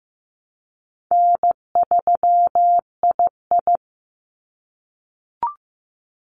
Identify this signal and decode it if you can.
Morse code, heard as a keyed tone, about 15 wpm, 700 Hz. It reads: N3II
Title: NA SOTA Activator Callsigns 15wpm